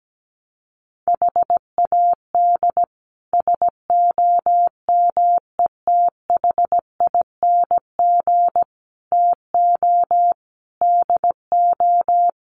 Morse code, heard as a keyed tone, about 17 wpm, 700 Hz. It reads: HAD SOMETHING TO DO
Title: Sets of 5 Words 17wpm